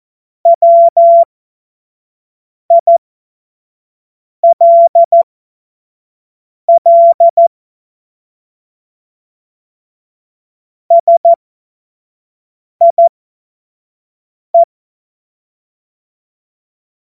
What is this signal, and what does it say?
Morse code, heard as a keyed tone, about 14 wpm, spaced as for 5 wpm, 680 Hz. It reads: WILL SIE
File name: Morsekurs / Witze / Witz 01. Februar 2022 25 F70